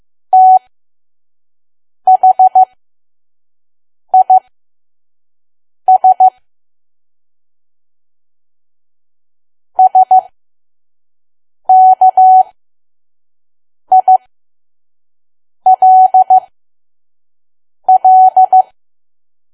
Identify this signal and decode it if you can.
Morse code, heard as a keyed tone, about 15 wpm, spaced as for 5 wpm, 750 Hz. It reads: THIS SKILL
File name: Archive / 5 WPM files / PM